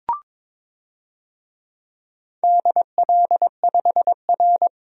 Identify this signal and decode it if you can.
Morse code, heard as a keyed tone, about 22 wpm, 700 Hz. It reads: DL5R
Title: DXCC-100 Call Signs - 4 Characters - 22wpm